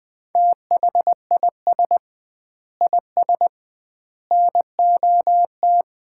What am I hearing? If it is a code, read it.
Morse code, heard as a keyed tone, about 20 wpm, 700 Hz. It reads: THIS IS NOT
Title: 2x Word-Spacing - Sets of 3 Words - 20wpm